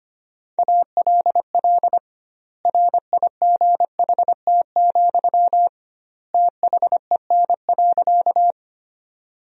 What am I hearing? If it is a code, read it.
Morse code, heard as a keyed tone, about 25 wpm, 700 Hz. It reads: ALL RIGHT, THEN.
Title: 2x Word-Spacing - Sentences from Top 200 Words - 25wpm